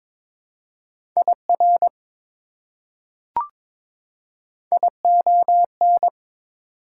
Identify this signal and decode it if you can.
Morse code, heard as a keyed tone, about 22 wpm, 700 Hz. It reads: IR ION
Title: Prefixes and Suffixes 22wpm